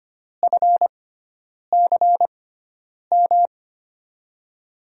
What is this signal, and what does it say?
Morse code, heard as a keyed tone, about 25 wpm, spaced as for 9 wpm, 700 Hz. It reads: FCM
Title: USA Call Signs - 6x Character Spacing - 25wpm